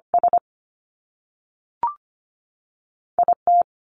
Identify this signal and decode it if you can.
Morse code, heard as a keyed tone, about 25 wpm, 700 Hz. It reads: S IT
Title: Top Two Letter Combinations in Words 25wpm